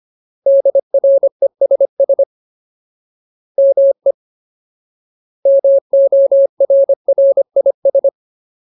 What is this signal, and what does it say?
Morse code, heard as a keyed tone, about 25 wpm, 550 Hz. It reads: DRESS ME MORRIS
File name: Ultimate Word List / 25WPM Ultimate